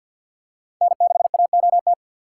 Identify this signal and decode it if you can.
Morse code, heard as a keyed tone, about 52 wpm, 700 Hz. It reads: N6AOT
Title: USA Call Signs - 5 Characters - 6x Character Spacing - 50wpm